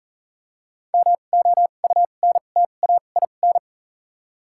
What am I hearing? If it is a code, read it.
Morse code, heard as a keyed tone, about 40 wpm, spaced as for 29 wpm, 700 Hz. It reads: MOUNTAIN
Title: 2x Character Spacing - Top 500 Words - 40wpm